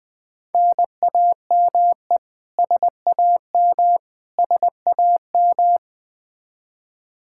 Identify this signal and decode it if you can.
Morse code, heard as a keyed tone, about 20 wpm, 700 Hz. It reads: NAME SAM SAM
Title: Practice QSO - Exchange 1 - Beginning - 20wpm